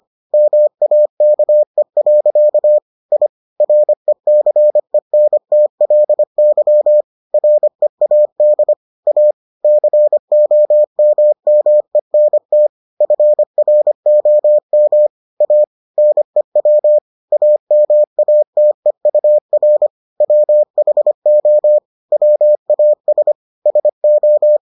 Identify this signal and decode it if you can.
Morse code, heard as a keyed tone, about 25 wpm, 600 Hz. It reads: MAKE. I RECENTLY READ A COMMENT FROM A NEW AMATEUR WHO WAS SO